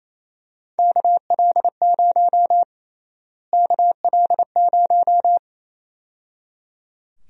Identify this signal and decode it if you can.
Morse code, heard as a keyed tone, about 27 wpm, 700 Hz. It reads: KL0 KL0
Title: USA Call Sign Prefixes - Repeated 2x - 28wpm